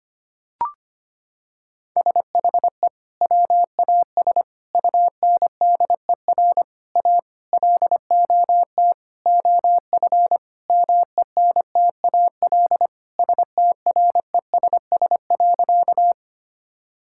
Morse code, heard as a keyed tone, about 25 wpm, 700 Hz. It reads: SHE WAS UNDER A LOT OF MENTAL STRESS.
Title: Sentences from Top 1000 Words - Review Part 6 - 25wpm